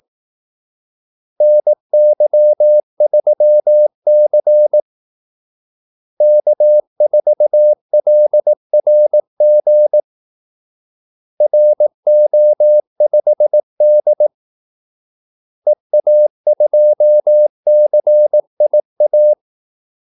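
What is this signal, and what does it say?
Morse code, heard as a keyed tone, about 18 wpm, 600 Hz. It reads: NY3C K4LRG RO5D EA2CIA
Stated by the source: Morse call-sign practice at 18 wpm.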